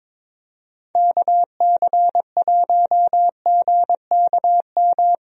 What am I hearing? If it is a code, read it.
Morse code, heard as a keyed tone, about 22 wpm, 700 Hz. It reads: KC1GKM
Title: USA Call Signs - 6 Characters - Repeated 2x - 22wpm